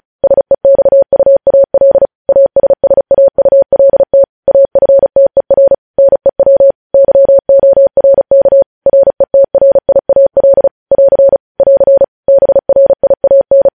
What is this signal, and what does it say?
Morse code, heard as a keyed tone, about 36 wpm, 550 Hz. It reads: SEXUAL ASSAULT AFTER NEW YORK RETRIAL <AR> <AR> BRIAN
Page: news headlines in morse code at 35 wpm